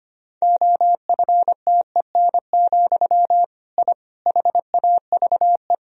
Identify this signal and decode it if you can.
Morse code, heard as a keyed tone, about 25 wpm, 700 Hz. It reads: OFTEN, I HAVE
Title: Sentences from Top 1000 Words - Review Part 1 - 25wpm